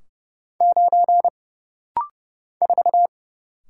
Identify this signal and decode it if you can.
Morse code, heard as a keyed tone, about 30 wpm, 700 Hz. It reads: 9 4
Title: Single Letter-Number - Rapid-Fire 30wpm